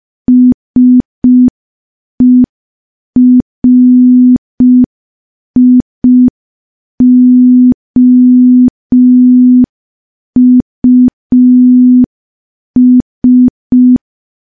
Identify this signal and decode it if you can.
Morse code, heard as a keyed tone, about 5 wpm, 250 Hz. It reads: SERIOUS
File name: morse / morse-training-05WPM